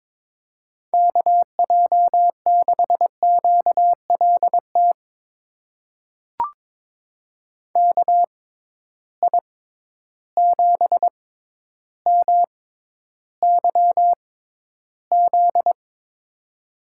Morse code, heard as a keyed tone, about 22 wpm, spaced as for 11 wpm, 700 Hz. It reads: KJ6QLT KI7MYZ
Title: USA Call Signs - 6 Characters - 6x Character Spacing - 22wpm